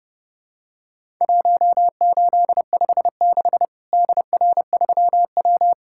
Morse code, heard as a keyed tone, about 30 wpm, 700 Hz. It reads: 1856 DR3W
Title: Worked All Europe DX Contest - QTC Exchange - 30wpm